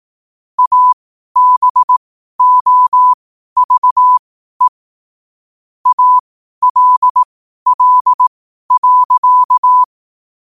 Morse code, heard as a keyed tone, about 18 wpm, spaced as for 12 wpm, 1000 Hz. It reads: ABOVE ALL.
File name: qotdfiles / 12wpm